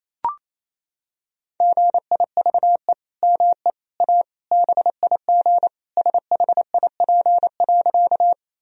Morse code, heard as a keyed tone, about 28 wpm, 700 Hz. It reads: GIVE ME A BIG SHIP.
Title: Sentences from Top 1000 Words - Review Part 2 - 28wpm